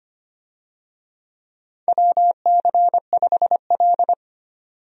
Morse code, heard as a keyed tone, about 25 wpm, 700 Hz. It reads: WC5L
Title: USA Call Signs - 4 Characters - 25wpm